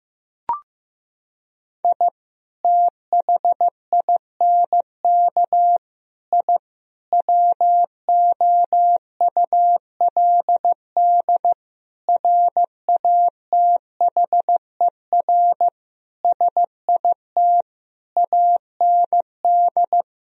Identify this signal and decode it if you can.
Morse code, heard as a keyed tone, about 15 wpm, 700 Hz. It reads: I THINK I WOULD RATHER SIT AND
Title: Sentences from Top 1000 Words - Review Part 4 - 15wpm